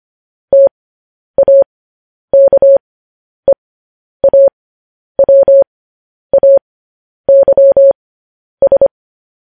Morse code, heard as a keyed tone, about 25 wpm, spaced as for 10 wpm, 550 Hz. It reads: TAKEAWAYS
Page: news-headlines-in-morse-code-at-10-wpm